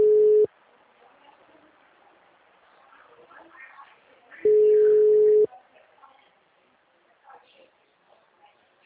Ringback tone (Europe, 425 Hz).